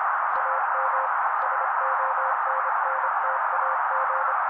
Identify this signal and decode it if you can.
Morse code, heard as a keyed tone, about 26 wpm, 530 Hz. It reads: AM SONNTAG
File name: daily / 25wpm noise